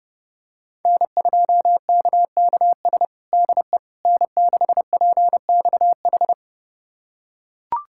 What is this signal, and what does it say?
Morse code, heard as a keyed tone, about 30 wpm, 700 Hz. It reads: N2KKS DE N6PXH
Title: USA Callsign DE Callsign - 5 Characters - 30wpm